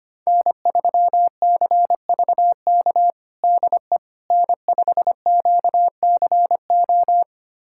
Morse code, heard as a keyed tone, about 25 wpm, 700 Hz. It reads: N3CVK DE N5QCO
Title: USA Callsign DE Callsign - 5 Characters - 25wpm